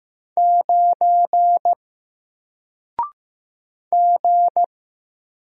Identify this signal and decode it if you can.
Morse code, heard as a keyed tone, about 15 wpm, 700 Hz. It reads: 9 G
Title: Single Letter-Number 15wpm